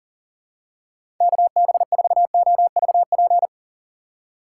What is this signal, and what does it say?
Morse code, heard as a keyed tone, about 40 wpm, 700 Hz. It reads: KB4OVP
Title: USA Call Signs - 6 Characters - 6x Character Spacing - 40wpm